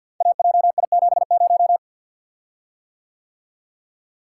Morse code, heard as a keyed tone, about 50 wpm, 700 Hz. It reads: AJI70
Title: Code Groups - 5 Letter:Numbers - 50wpm